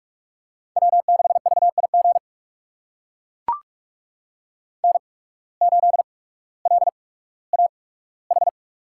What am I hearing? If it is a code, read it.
Morse code, heard as a keyed tone, about 44 wpm, spaced as for 19 wpm, 700 Hz. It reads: W6VIG N8LAH
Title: USA Call Signs - 5 Characters - 8x Character Spacing - 45wpm